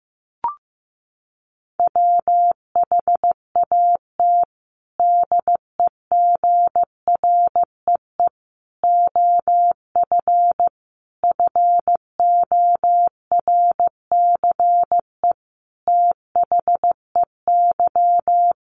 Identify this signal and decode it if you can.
Morse code, heard as a keyed tone, about 15 wpm, 700 Hz. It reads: WHAT DEGREE OF FORCE THEY
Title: Sentences from Top 1500 Words 15wpm